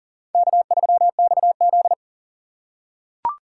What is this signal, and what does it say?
Morse code, heard as a keyed tone, about 40 wpm, 700 Hz. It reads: K3XZ
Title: USA Call Signs - 4 Characters - 2x Character Spacing - 40wpm